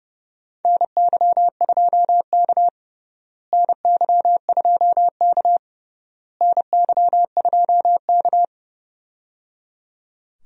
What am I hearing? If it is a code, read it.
Morse code, heard as a keyed tone, about 30 wpm, 700 Hz. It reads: NY2K NY2K NY2K
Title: USA Call Signs - 4 Characters - Repeated 3x - 30wpm